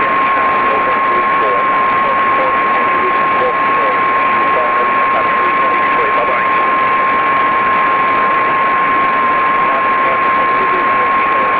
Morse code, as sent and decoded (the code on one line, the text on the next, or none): none